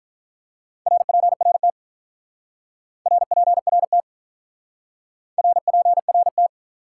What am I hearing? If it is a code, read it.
Morse code, heard as a keyed tone, about 47 wpm, 700 Hz. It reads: RPRT RPRT RPRT